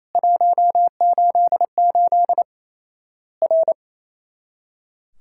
Morse code, heard as a keyed tone, about 28 wpm, 700 Hz. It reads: 188 R
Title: Worked All Europe DX Contest - QTC Exchange - 28wpm